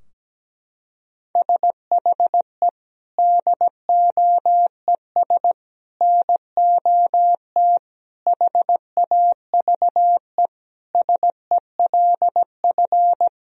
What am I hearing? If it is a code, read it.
Morse code, heard as a keyed tone, about 17 wpm, 700 Hz. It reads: SHE DOES NOT HAVE SELF
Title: Sentences from Top 1000 Words - Review Part 3 - 17wpm